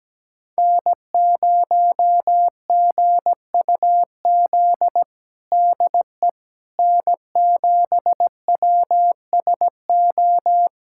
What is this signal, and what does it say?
Morse code, heard as a keyed tone, about 17 wpm, 700 Hz. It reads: N0GUZ DE N7WSO
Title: USA Callsign DE Callsign - 5 Characters - 17wpm